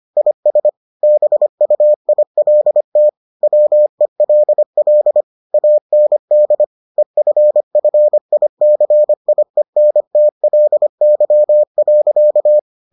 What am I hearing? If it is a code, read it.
Morse code, heard as a keyed tone, about 25 wpm, 600 Hz. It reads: IS BUILT WELL AND EFFICIENTLY.